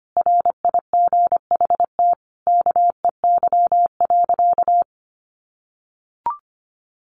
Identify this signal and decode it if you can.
Morse code, heard as a keyed tone, about 25 wpm, 700 Hz. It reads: RIGHT KEY.
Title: Sentences from Top 600 Words 25wpm